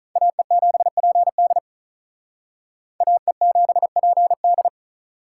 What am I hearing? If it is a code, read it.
Morse code, heard as a keyed tone, about 43 wpm, 700 Hz. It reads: AE7PD AE7PD
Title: Speed Racing - USA 2x2 Call Signs 35wpm